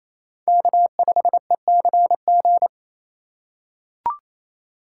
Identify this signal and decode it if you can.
Morse code, heard as a keyed tone, about 28 wpm, 700 Hz. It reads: K5ECG
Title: USA Call Signs - 5 Characters - 4x Character Spacing - 28wpm